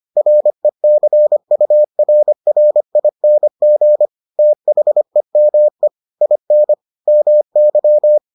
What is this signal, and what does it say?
Morse code, heard as a keyed tone, about 25 wpm, 600 Hz. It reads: RECURRING THEME IN MY